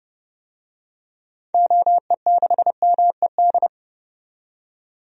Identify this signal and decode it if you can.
Morse code, heard as a keyed tone, about 32 wpm, 700 Hz. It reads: OE6MED
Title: DXCC-100 Call Signs - 6 Characters - 30wpm